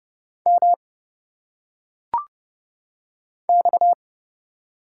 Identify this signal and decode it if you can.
Morse code, heard as a keyed tone, about 30 wpm, 700 Hz. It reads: M X
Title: Single Letters - Phonetic Pronunciation 30wpm